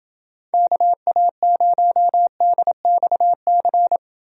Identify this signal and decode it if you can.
Morse code, heard as a keyed tone, about 29 wpm, 700 Hz. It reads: KA0DXC